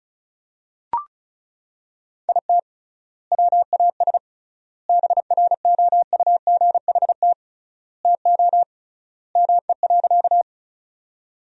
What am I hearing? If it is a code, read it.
Morse code, heard as a keyed tone, about 36 wpm, 700 Hz. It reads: IT WAS BROUGHT TO ME.
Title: 3x Word-Spacing - Sentences from Top 500 Words - 35wpm